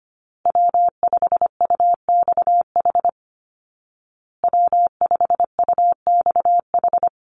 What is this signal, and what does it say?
Morse code, heard as a keyed tone, about 25 wpm, 700 Hz. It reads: W5UXH W5UXH
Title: Call Signs Repeated 2x - 25wpm